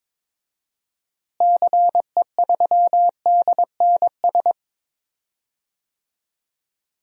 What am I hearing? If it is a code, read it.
Morse code, heard as a keyed tone, about 21 wpm, 700 Hz. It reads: CE3DNS
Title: DXCC-100 Call Signs - 6 Characters - Repeated 2x - 22wpm